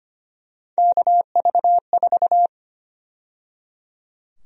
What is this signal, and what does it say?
Morse code, heard as a keyed tone, about 25 wpm, 700 Hz. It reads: KV4
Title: USA Call Sign Prefixes - 25wpm